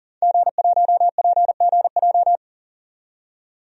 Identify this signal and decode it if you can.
Morse code, heard as a keyed tone, about 40 wpm, 700 Hz. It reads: G1PGJ
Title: DXCC-100 Call Signs - 5 Characters - 40wpm